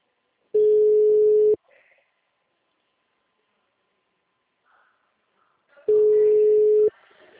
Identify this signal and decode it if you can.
Ringback tone (Europe, 425 Hz).
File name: out- -2017- - - .127